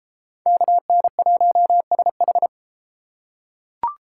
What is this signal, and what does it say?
Morse code, heard as a keyed tone, about 32 wpm, 700 Hz. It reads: KN1SH